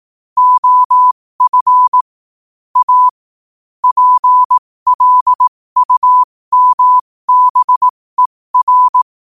Morse code, heard as a keyed tone, about 18 wpm, spaced as for 15 wpm, 1000 Hz. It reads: OF A PLUMBER
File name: qotdfiles / 15wpm